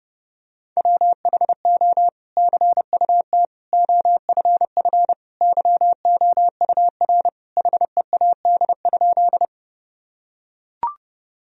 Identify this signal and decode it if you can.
Morse code, heard as a keyed tone, about 30 wpm, 700 Hz. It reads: WHO CUT OFF YOUR HEAD?
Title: Sentences from Top 1000 Words - Review Part 1 - 30wpm